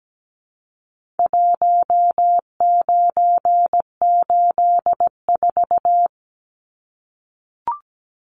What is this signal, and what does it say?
Morse code, heard as a keyed tone, about 17 wpm, 700 Hz. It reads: 1984
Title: Numbers - Contest Style - 4-Digit 17wpm